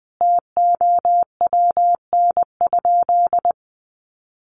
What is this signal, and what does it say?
Morse code, heard as a keyed tone, about 20 wpm, 700 Hz. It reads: TOWN?
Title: Sentences from Top 1300 Words 20wpm